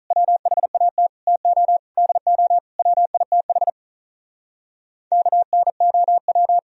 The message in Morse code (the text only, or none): WHAT TO DO WITH KNOW